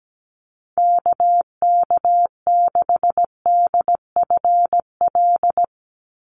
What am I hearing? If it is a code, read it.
Morse code, heard as a keyed tone, about 17 wpm, 700 Hz. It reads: KK6DFL
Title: USA Call Signs 17wpm